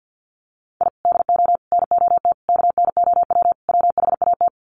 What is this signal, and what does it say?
Morse code, heard as a keyed tone, about 50 wpm, 700 Hz. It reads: I DO NOT KNOW WHAT
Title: Sets of 5 Words 50wpm